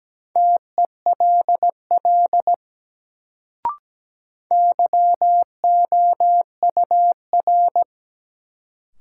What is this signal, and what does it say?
Morse code, heard as a keyed tone, about 17 wpm, 700 Hz. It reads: TELL YOUR
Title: Sets of 2 Words - New Word Drill - 17wpm